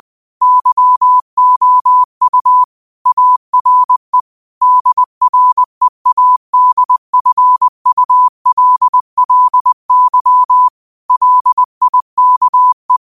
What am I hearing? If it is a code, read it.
Morse code, heard as a keyed tone, about 20 wpm, 1000 Hz. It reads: YOU ARE DREADFULLY LIKE